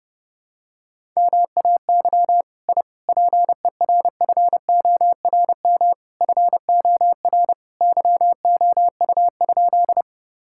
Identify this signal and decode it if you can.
Morse code, heard as a keyed tone, about 32 wpm, 700 Hz. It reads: MAY I PERFORM FOR YOU?
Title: Sentences from Top 1000 Words - Review Part 5 - 30wpm